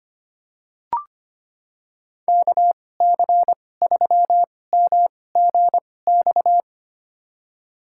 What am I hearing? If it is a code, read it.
Morse code, heard as a keyed tone, about 25 wpm, spaced as for 18 wpm, 700 Hz. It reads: KC3MGX
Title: USA Call Signs - 6 Characters - 2x Character Spacing - 25wpm